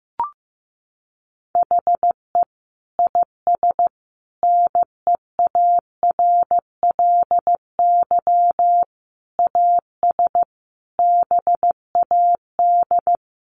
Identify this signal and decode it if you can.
Morse code, heard as a keyed tone, about 15 wpm, 700 Hz. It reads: HE IS NEARLY AS BAD
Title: Sentences from Top 1300 Words 15wpm